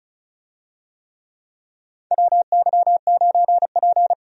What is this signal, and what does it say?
Morse code, heard as a keyed tone, about 34 wpm, 700 Hz. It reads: WY9P
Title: USA Call Signs - 4 Characters - 35wpm